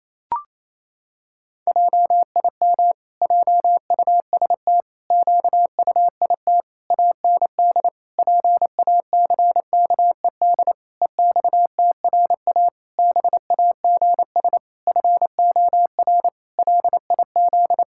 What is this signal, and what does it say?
Morse code, heard as a keyed tone, about 28 wpm, 700 Hz. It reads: JIM JUST QUIT AND PACKED EXTRA BAGS FOR LIZ
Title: Pangram Challenge v2 - 28wpm